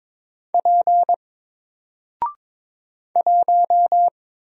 Morse code, heard as a keyed tone, about 22 wpm, 700 Hz. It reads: P 1
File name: Single-Letter-Number-22wpm